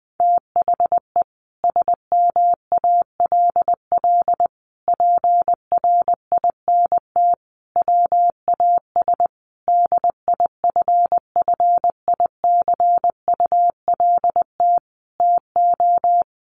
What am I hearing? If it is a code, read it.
Morse code, heard as a keyed tone, about 20 wpm, 700 Hz. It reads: THE SMALL PRINT WAS DIFFICULT TO
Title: Sentences from Top 1400 Words 20wpm